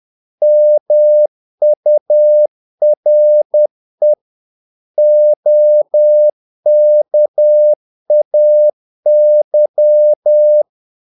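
Morse code, heard as a keyed tone, about 10 wpm, 600 Hz. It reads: MURE OKAY
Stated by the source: Morse practice of short Danish words at 10 wpm, from CW med Gnister.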